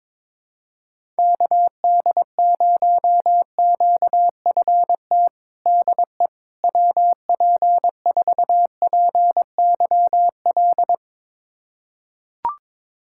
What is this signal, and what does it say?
Morse code, heard as a keyed tone, about 22 wpm, 700 Hz. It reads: KD0QFT DE WP4PYL
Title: USA Callsign DE Callsign - 6 Characters - 22wpm